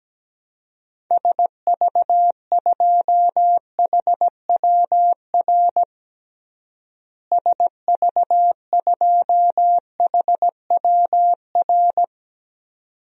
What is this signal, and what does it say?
Morse code, heard as a keyed tone, about 17 wpm, 700 Hz. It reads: SV2HWR SV2HWR
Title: DXCC-100 Call Signs - 6 Characters - Repeated 2x - 17wpm